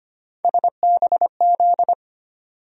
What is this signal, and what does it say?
Morse code, heard as a keyed tone, about 25 wpm, 700 Hz. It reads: SBZ
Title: USA Call Sign Postfixes - 3 Letters 25wpm